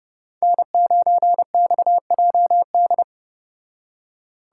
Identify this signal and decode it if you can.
Morse code, heard as a keyed tone, about 30 wpm, 700 Hz. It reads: N9XJD